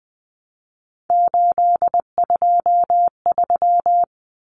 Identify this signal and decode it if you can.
Morse code, heard as a keyed tone, about 20 wpm, 700 Hz. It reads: 823
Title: Common Cut Numbers - 3-Digit 20wpm